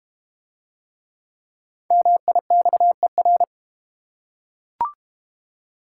Morse code, heard as a keyed tone, about 30 wpm, 690 Hz. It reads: MIXER
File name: Easy Words - Speed-Racing with Hints - 25wpm